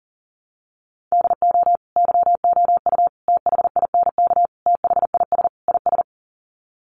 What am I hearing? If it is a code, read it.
Morse code, heard as a keyed tone, about 37 wpm, 700 Hz. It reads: DO YOU THINK THIS IS
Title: Sets of 5 Words Encore2 40wpm